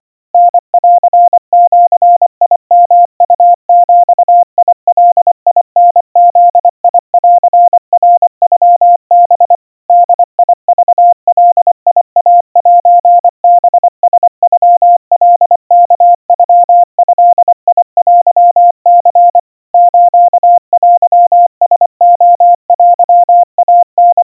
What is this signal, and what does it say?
Morse code (Japanese Wabun code), heard as a keyed tone, about 24 wpm, 700 Hz. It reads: タンジヨウビガタブンナノハ ボクガイセハラノカワノドテニ ステラレテイタ
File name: 90 cpm 700 Hz Crispy